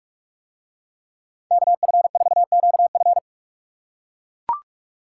Morse code, heard as a keyed tone, about 41 wpm, 700 Hz. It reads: KF4QF